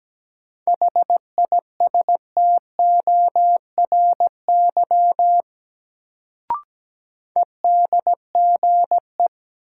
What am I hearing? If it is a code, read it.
Morse code, heard as a keyed tone, about 17 wpm, 700 Hz. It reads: HISTORY EDGE
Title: Sets of 3 Words - New Word Drill - 17wpm